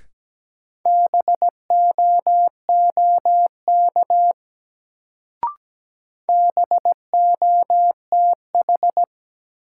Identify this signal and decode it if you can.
Morse code, heard as a keyed tone, about 17 wpm, 700 Hz. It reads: BOOK BOTH
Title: Sentences from Top 300 Words - New Word Drill - 17wpm